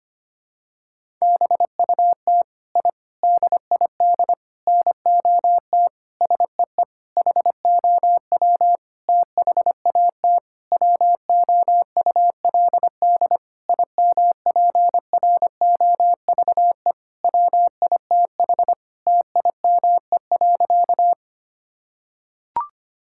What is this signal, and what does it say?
Morse code, heard as a keyed tone, about 25 wpm, 700 Hz. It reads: BUT I DID NOT SEE HOW THAT WOULD IMPROVE WITH TIME.
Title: Sentences from Top 1000 Words - Review Part 5 - 25wpm